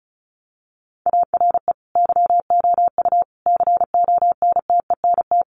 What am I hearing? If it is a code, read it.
Morse code, heard as a keyed tone, about 34 wpm, 700 Hz. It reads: ARE YOU CONTENT
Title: Sentences from Top 900 Words 35wpm